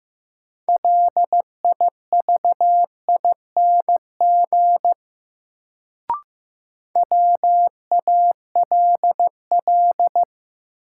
Morse code, heard as a keyed tone, about 15 wpm, 700 Hz. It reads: LIVING WALL
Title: Sentences from Top 1100 Words - New Word Drill Alt - 15wpm